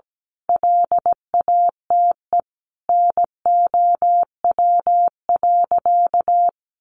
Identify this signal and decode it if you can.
Morse code, heard as a keyed tone, about 17 wpm, 700 Hz. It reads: LATE NOW.
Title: Sentences from Top 300 Words 17wpm